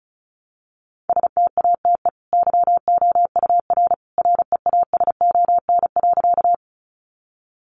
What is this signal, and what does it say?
Morse code, heard as a keyed tone, about 34 wpm, 700 Hz. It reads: STATE YOUR REASON.
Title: Sentences from Top 700 Words 35wpm